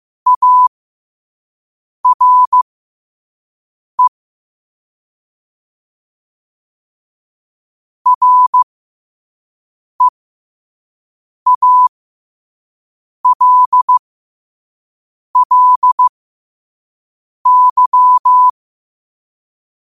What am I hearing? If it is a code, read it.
Morse code, heard as a keyed tone, about 15 wpm, spaced as for 5 wpm, 1000 Hz. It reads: ARE REALLY
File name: qotdfiles / 5wpm